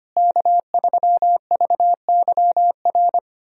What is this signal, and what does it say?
Morse code, heard as a keyed tone, about 25 wpm, 700 Hz. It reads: K3VYR
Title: USA Call Signs - 5 Characters - Repeated 3x - 25wpm